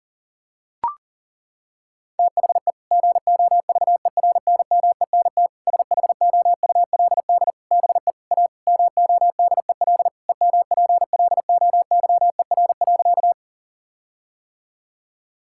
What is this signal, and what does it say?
Morse code, heard as a keyed tone, about 40 wpm, 700 Hz. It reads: THE GOVERNMENT SHOULD BE A MODEL EMPLOYER.
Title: Sentences from Top 1600 Words 40wpm